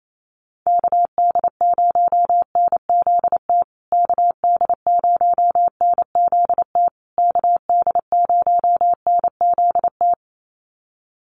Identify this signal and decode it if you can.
Morse code, heard as a keyed tone, about 28 wpm, 700 Hz. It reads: KD0NZT KD0NZT KD0NZT
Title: USA Call Signs Repeated 3x 28wpm